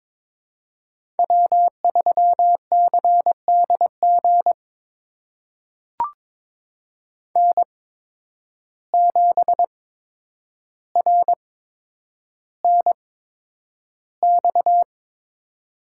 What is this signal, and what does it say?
Morse code, heard as a keyed tone, about 22 wpm, spaced as for 9 wpm, 700 Hz. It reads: W3CDG N7RNX
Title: USA Call Signs - 8x Character Spacing - 22wpm